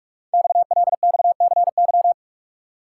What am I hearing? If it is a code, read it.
Morse code, heard as a keyed tone, about 46 wpm, 690 Hz. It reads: XLXCY
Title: Code Groups - 5 Letters - 45wpm